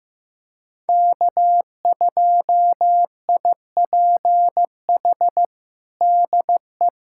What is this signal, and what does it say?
Morse code, heard as a keyed tone, about 16 wpm, 700 Hz. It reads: K2IPH DE
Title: USA Callsign DE Callsign - 5 Characters - 15wpm